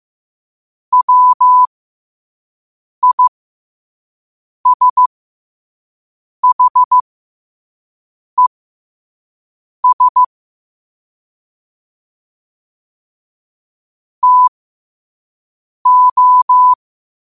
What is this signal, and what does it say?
Morse code, heard as a keyed tone, about 15 wpm, spaced as for 5 wpm, 1000 Hz. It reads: WISHES TO
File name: qotdfiles / 5wpm